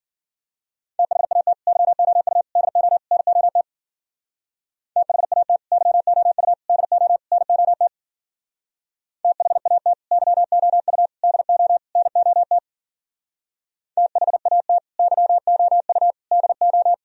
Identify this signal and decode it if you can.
Morse code, heard as a keyed tone, about 54 wpm, 700 Hz. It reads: THAT YOU DO NOT THAT YOU DO NOT THAT YOU DO NOT THAT YOU DO